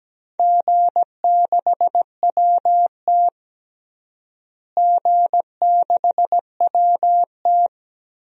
Morse code, heard as a keyed tone, about 17 wpm, 700 Hz. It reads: G6WT G6WT
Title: DXCC-100 Call Signs - 4 Characters - Repeated 2x - 17wpm